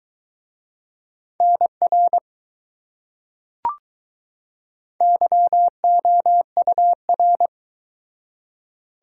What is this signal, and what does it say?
Morse code, heard as a keyed tone, about 20 wpm, 700 Hz. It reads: NR YOUR